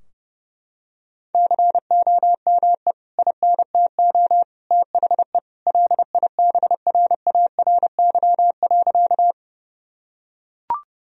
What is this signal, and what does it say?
Morse code, heard as a keyed tone, about 30 wpm, 700 Hz. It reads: COME INTO THE LIBRARY.
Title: Sentences from Top 1000 Words - Review Part 4 - 30wpm